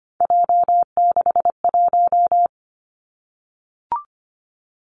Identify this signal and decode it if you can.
Morse code, heard as a keyed tone, about 25 wpm, 700 Hz. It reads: J61
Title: Three Challenging Letter-Number Combinations 25wpm